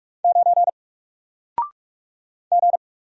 Morse code, heard as a keyed tone, about 44 wpm, 700 Hz. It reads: 9 G
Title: Single Character (Letter:Number:Punctuation:Prosign) 45wpm